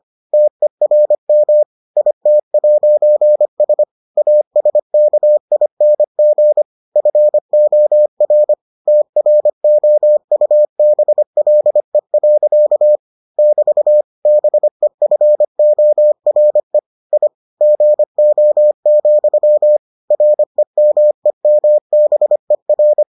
Morse code, heard as a keyed tone, about 25 wpm, 600 Hz. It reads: TERM IT'S ASKING FOR TROUBLE. <BT> BEFORE I GO, REMEMBER